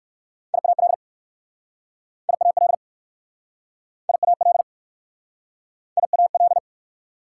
Morse code, heard as a keyed tone, about 59 wpm, 690 Hz. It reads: IAL IAL IAL IAL